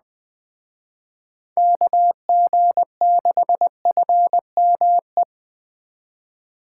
Morse code, heard as a keyed tone, about 20 wpm, 700 Hz. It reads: KG6FME
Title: Speed Racing - USA 2x3 Call Signs 17wpm